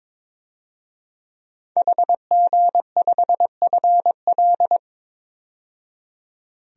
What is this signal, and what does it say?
Morse code, heard as a keyed tone, about 22 wpm, 700 Hz. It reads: HG5FL
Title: DXCC-100 Call Signs - 5 Characters - 22wpm